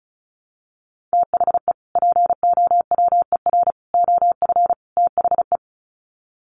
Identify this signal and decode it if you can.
Morse code, heard as a keyed tone, about 34 wpm, 700 Hz. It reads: THE POWER OF THE
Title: Sets of 4 Words 35wpm